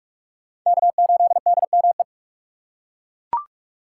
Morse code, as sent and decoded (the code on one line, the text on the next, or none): -.- ---.. -.. -- .
K8DME